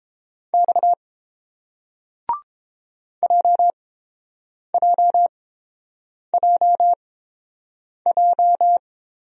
Morse code, heard as a keyed tone, about 30 wpm, spaced as for 7 wpm, 700 Hz. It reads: X JJJJ